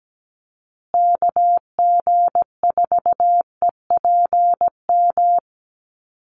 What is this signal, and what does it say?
Morse code, heard as a keyed tone, about 17 wpm, 700 Hz. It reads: KG4EPM
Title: USA Call Signs Repeated 2x 17wpm